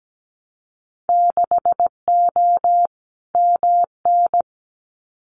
Morse code, heard as a keyed tone, about 17 wpm, 700 Hz. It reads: 6O MN
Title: Winter Field Day Exchanges Encore 17wpm